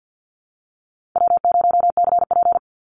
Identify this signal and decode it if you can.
Morse code, heard as a keyed tone, about 48 wpm, 700 Hz. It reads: W0CP
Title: NA SOTA Activator Callsigns 50wpm